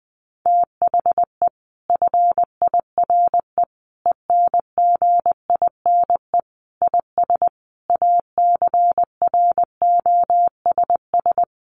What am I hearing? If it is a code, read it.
Morse code, heard as a keyed tone, about 20 wpm, 700 Hz. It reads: THE FIRE ENGINE IS ACROSS
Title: Sentences from Top 1500 Words 20wpm